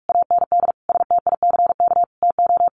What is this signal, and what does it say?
Morse code, heard as a keyed tone, about 44 wpm, 700 Hz. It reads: AND STICK TO
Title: Sentences from Top 1200 Words 45wpm